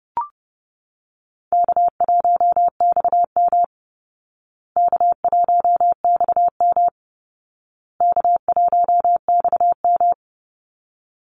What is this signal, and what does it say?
Morse code, heard as a keyed tone, about 30 wpm, 700 Hz. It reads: K1XM K1XM K1XM
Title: Easy Call Signs Repeated 3x - 30wpm